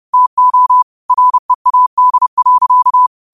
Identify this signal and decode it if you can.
Morse code, heard as a keyed tone, about 30 wpm, 1000 Hz. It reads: TO READ.